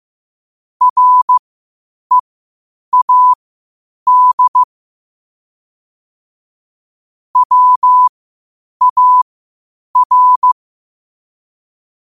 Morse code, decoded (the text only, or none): READ WAR